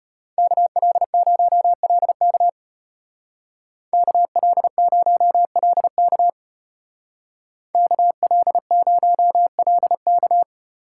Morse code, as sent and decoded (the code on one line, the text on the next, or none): -.- .-.. ----- .-.. -.- / -.- .-.. ----- .-.. -.- / -.- .-.. ----- .-.. -.-
KL0LK KL0LK KL0LK